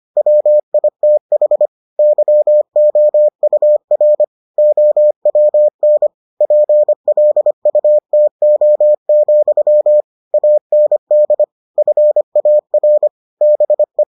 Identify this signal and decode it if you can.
Morse code, heard as a keyed tone, about 25 wpm, 600 Hz. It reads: WITH YOUR OWN PLUTO, AND FAR BE